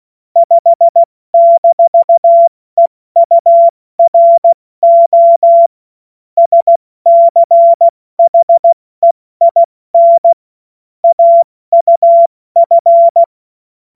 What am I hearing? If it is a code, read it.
Morse code, heard as a keyed tone, about 16 wpm, 680 Hz. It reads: 5-EURO SCHEIN AUF
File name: Morsekurs / Witze / Witz 11. Mai 2021, 70 F80